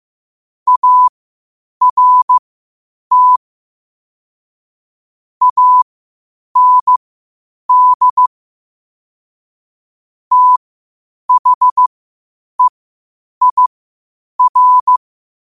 Morse code, heard as a keyed tone, about 15 wpm, spaced as for 8 wpm, 1000 Hz. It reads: ART AND THEIR
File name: qotdfiles / 8wpm